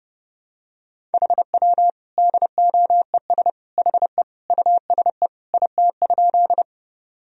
Morse code, heard as a keyed tone, about 30 wpm, 700 Hz. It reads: HW DOES HE USE IT?
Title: Sentences from Top 200 Words and Abbreviations 30wpm